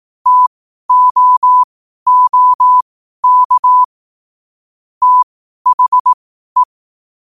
Morse code, heard as a keyed tone, about 18 wpm, spaced as for 12 wpm, 1000 Hz. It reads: TOOK THE
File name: qotdfiles / 12wpm